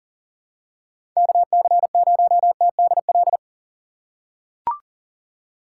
Morse code, heard as a keyed tone, about 39 wpm, 700 Hz. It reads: KC0TDL
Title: USA Call Signs - 6 Characters - 3x Character Spacing - 40wpm